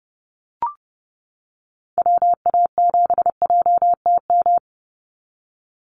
Morse code, heard as a keyed tone, about 30 wpm, 700 Hz. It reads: WA7JTM
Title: NA SOTA Activator Callsigns 30wpm